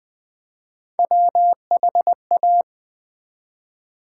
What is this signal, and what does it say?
Morse code, heard as a keyed tone, about 20 wpm, 700 Hz. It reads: WHA
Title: Build A Word - 3 Letters At A Time - 8-Letter Words - 20wpm